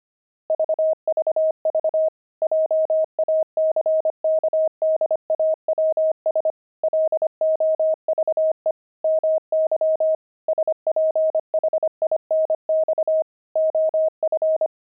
Morse code, heard as a keyed tone, about 25 wpm, 630 Hz. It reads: VVV JACKDAWS LOVE MY SPHINX OF
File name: pangram-fs-25-cs-25-love